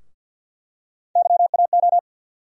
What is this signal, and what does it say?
Morse code, heard as a keyed tone, about 50 wpm, 700 Hz. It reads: YAO
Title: USA Call Sign Postfixes - 3 Letters 50wpm